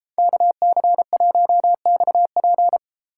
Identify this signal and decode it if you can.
Morse code, heard as a keyed tone, about 34 wpm, 700 Hz. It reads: KC1XP
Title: Speed Racing - USA 2x2 Call Signs 28wpm